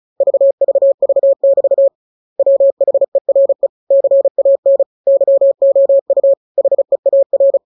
Morse code, heard as a keyed tone, about 36 wpm, 550 Hz. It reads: VVV<BT> WHERE CAN YOU HEAR